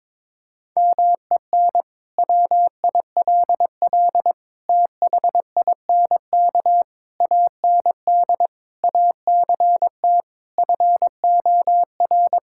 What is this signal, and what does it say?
Morse code, heard as a keyed tone, about 22 wpm, 700 Hz. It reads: MEN WILL THINK AND ACT FOR
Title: Sentences from Top 1000 Words - Review Part 5 - 22wpm